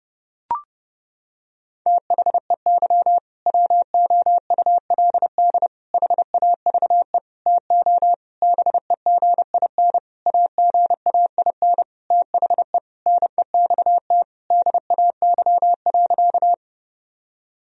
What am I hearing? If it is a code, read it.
Morse code, heard as a keyed tone, about 30 wpm, 700 Hz. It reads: THEY WOULD HAVE TO BEGIN AGAIN THE NEXT DAY.
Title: Sentences from Top 1000 Words - Review Part 1 - 30wpm